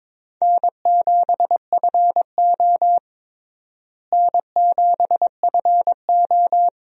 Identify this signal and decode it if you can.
Morse code, heard as a keyed tone, about 22 wpm, 700 Hz. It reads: N7FO N7FO
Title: USA Call Signs - 4 Characters - Repeated 2x - 22wpm